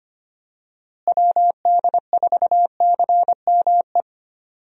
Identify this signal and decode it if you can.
Morse code, heard as a keyed tone, about 25 wpm, 700 Hz. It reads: WD4CME
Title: USA Call Signs - 6 Characters - Repeated 3x - 25wpm